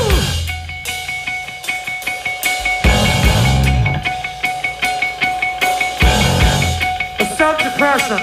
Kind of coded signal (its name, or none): dial-up modem handshake